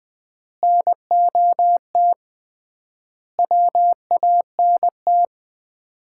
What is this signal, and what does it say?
Morse code, heard as a keyed tone, about 21 wpm, 700 Hz. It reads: NOT WANT
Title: 3x Word-Spacing - Sets of 4 Words - 20wpm